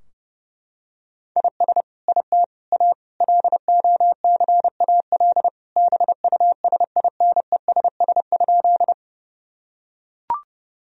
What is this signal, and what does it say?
Morse code, heard as a keyed tone, about 30 wpm, 700 Hz. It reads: IS IT A LOCAL BUSINESS?
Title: Sentences from Top 1000 Words - Review Part 3 - 30wpm